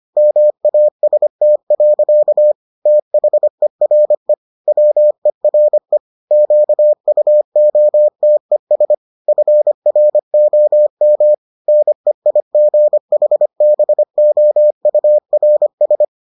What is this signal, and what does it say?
Morse code, heard as a keyed tone, about 25 wpm, 600 Hz. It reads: MAST. THERE WERE QUOTES FROM NEIGHBOURS